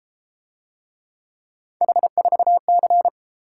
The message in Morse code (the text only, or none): H4C